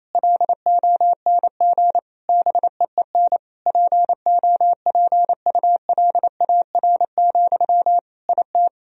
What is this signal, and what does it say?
Morse code, heard as a keyed tone, about 27 wpm, 700 Hz. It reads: LONG BEEN POPULAR, IT
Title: Sentences from Top 1000 Words - Review Part 3 - 28wpm